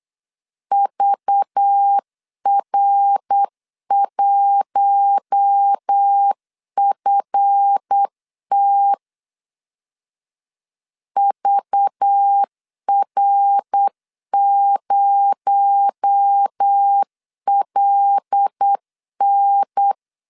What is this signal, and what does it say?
Morse code, heard as a keyed tone, about 8 wpm, 800 Hz. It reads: VR1FT VR0LN
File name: LEZIONI / 7 GF9Z / 7 GF 8wpm 3108